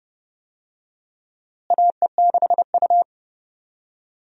Morse code, heard as a keyed tone, about 30 wpm, 700 Hz. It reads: AE6U
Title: USA Call Signs - 4 Characters - Repeated 2x - 30wpm